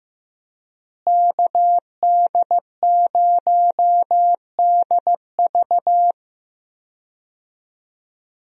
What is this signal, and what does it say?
Morse code, heard as a keyed tone, about 15 wpm, 700 Hz. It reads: KD0DV
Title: USA Call Signs - 5 Characters - Repeated 3x - 15wpm